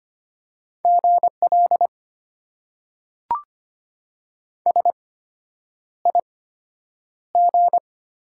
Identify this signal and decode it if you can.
Morse code, heard as a keyed tone, about 25 wpm, spaced as for 9 wpm, 700 Hz. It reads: GL SIG
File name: Top 90 QSO Elements - 8x Character Spacing - 25wpm